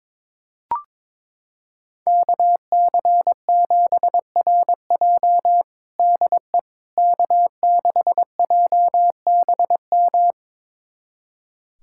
Morse code, heard as a keyed tone, about 22 wpm, 700 Hz. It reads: KC7RJ DE K6JBM
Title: USA Callsign DE Callsign - 5 Characters - 22wpm